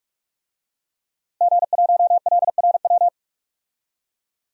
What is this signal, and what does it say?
Morse code, heard as a keyed tone, about 43 wpm, 700 Hz. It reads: G1LRW
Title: DXCC-100 Call Signs - 5 Characters - Repeated 3x - 45wpm